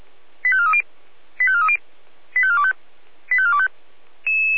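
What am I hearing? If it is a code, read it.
Five-tone selective call (ZVEI), heard as quick runs of five tones, tones of about 70 ms. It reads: 85430 85430 85425 85425
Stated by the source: ZVEI five-tone selective call received on SDR.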